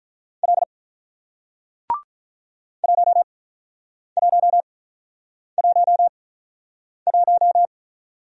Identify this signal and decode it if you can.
Morse code, heard as a keyed tone, about 48 wpm, spaced as for 9 wpm, 700 Hz. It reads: L 1111